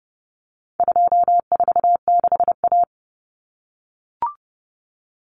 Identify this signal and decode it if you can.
Morse code, heard as a keyed tone, about 30 wpm, 700 Hz. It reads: 246A
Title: Common Cut Numbers - 4-Digit 30wpm